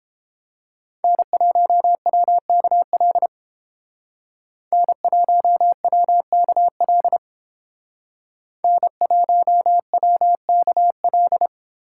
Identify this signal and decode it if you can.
Morse code, heard as a keyed tone, about 32 wpm, 700 Hz. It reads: N1WKL N1WKL N1WKL